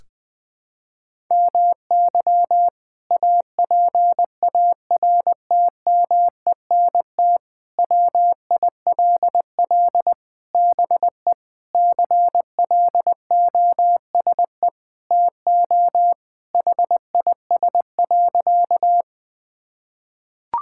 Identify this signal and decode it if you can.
Morse code, heard as a keyed tone, about 20 wpm, 700 Hz. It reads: MY APARTMENT WILL BE CLOSE TO HIS.
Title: Sentences from Top 1000 Words - Review Part 6 - 20wpm